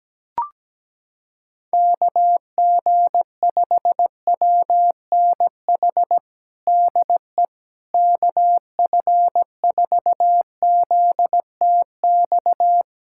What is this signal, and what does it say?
Morse code, heard as a keyed tone, about 17 wpm, 700 Hz. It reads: KG5WNH DE KF4ZTX
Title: USA Callsign DE Callsign 17wpm